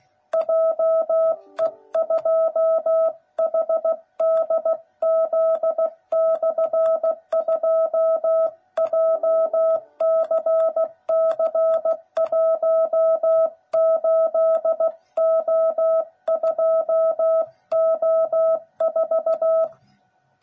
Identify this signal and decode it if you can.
Morse code, heard as a keyed tone, about 16 wpm, 660 Hz. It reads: JE2HDZ/2 JCC18O2O4
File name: jh8cbh / 2021cw / cw09b